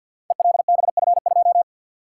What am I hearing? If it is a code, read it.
Morse code, heard as a keyed tone, about 47 wpm, 700 Hz. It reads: ELBF2
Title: Code Groups - 5 Letter:Numbers - 50wpm